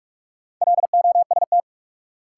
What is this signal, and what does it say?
Morse code, heard as a keyed tone, about 48 wpm, 700 Hz. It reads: LOST
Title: Sentences from Top 1400 Words - New Word Drill Alt - 45wpm